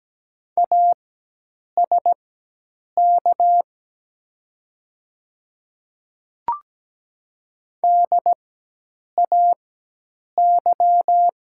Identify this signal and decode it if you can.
Morse code, heard as a keyed tone, about 17 wpm, spaced as for 8 wpm, 700 Hz. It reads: ASK DAY
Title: 4x Character Spacing - Top 200 Words - 17wpm